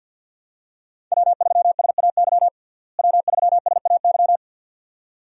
Morse code, heard as a keyed tone, about 51 wpm, 700 Hz. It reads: W3SAY W3SAY
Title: USA Call Signs - 5 Characters - Repeated 2x - 50wpm